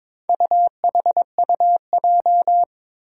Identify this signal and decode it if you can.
Morse code, heard as a keyed tone, about 22 wpm, 700 Hz. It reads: UHUJ
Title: Four Challenging Letter-Number Combinations 22wpm